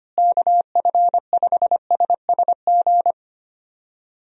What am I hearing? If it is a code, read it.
Morse code, heard as a keyed tone, about 25 wpm, 700 Hz. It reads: KF5SSG
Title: USA Call Signs - 6 Characters - Repeated 3x - 25wpm